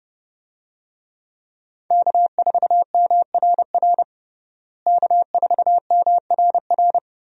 Morse code, heard as a keyed tone, about 30 wpm, 700 Hz. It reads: K4MRR K4MRR
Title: USA Call Signs - 5 Characters - Repeated 3x - 30wpm